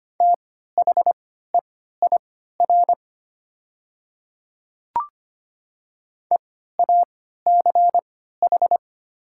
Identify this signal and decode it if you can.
Morse code, heard as a keyed tone, about 25 wpm, spaced as for 14 wpm, 700 Hz. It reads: THEIR EACH
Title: 3x Character Spacing - Top 100 Words - 25wpm